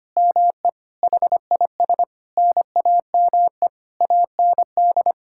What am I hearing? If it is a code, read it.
Morse code, heard as a keyed tone, about 25 wpm, 700 Hz. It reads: ME HIS NAME AND
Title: Sentences from Top 1000 Words - Review Part 6 - 25wpm